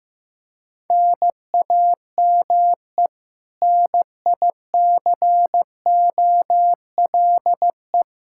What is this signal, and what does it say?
Morse code, heard as a keyed tone, about 16 wpm, 700 Hz. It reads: NAME NICOLE
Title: Practice QSO - Exchange 1 - Beginning - 15wpm